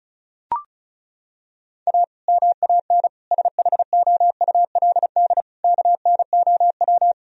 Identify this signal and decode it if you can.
Morse code, heard as a keyed tone, about 36 wpm, 700 Hz. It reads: A MAN SHOULD KNOW
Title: Sentences from Top 1000 Words - Review Part 2 - 35wpm